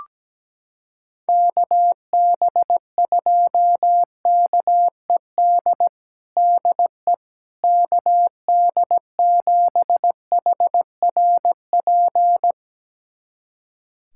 Morse code, heard as a keyed tone, about 17 wpm, 700 Hz. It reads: KB2KED DE KD7HRP
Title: USA Callsign DE Callsign 17wpm